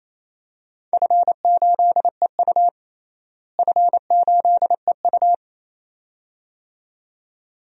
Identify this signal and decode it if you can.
Morse code, heard as a keyed tone, about 28 wpm, 700 Hz. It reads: F8EU F8EU
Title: DXCC-100 Call Signs - 4 Characters - Repeated 2x - 28wpm